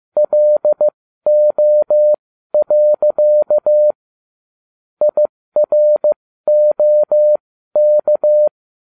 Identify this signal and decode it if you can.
Morse code, heard as a keyed tone, about 15 wpm, spaced as for 12 wpm, 600 Hz. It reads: LO. IROK
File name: cw / ubacw12 2